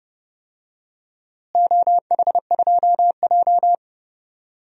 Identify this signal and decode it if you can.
Morse code, heard as a keyed tone, about 31 wpm, 700 Hz. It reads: OH2J
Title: DXCC-100 Call Signs - 4 Characters - Repeated 2x - 30wpm